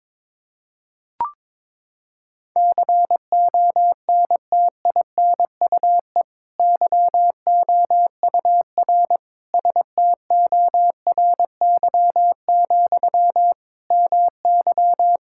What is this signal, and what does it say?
Morse code, heard as a keyed tone, about 22 wpm, 700 Hz. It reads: CONTINUE YOUR STORY, MY
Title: Sentences from Top 1000 Words - Review Part 6 - 22wpm